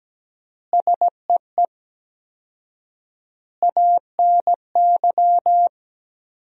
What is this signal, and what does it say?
Morse code, heard as a keyed tone, about 17 wpm, 700 Hz. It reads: SEE ANY
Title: 4x Word-Spacing - Sentences from Top 500 Words - 17wpm